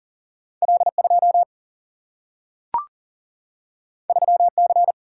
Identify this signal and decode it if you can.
Morse code, heard as a keyed tone, about 40 wpm, 700 Hz. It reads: L2 3C
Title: Single-Letter Single-Number 40wpm